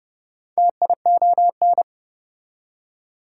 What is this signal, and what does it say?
Morse code, heard as a keyed tone, about 24 wpm, 700 Hz. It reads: TION
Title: Speed-Racing Prefixes and Suffixes 20wpm